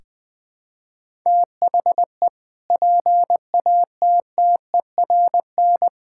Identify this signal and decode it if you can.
Morse code, heard as a keyed tone, about 20 wpm, 700 Hz. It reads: THE PATTERN
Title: Sentences with 6-Word Phrases - Spoken Beginning - 20wpm